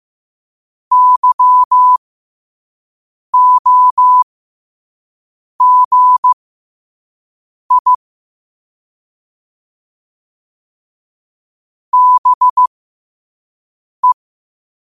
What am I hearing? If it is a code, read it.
Morse code, heard as a keyed tone, about 15 wpm, spaced as for 5 wpm, 1000 Hz. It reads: YOGI BE
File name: qotdfiles / 5wpm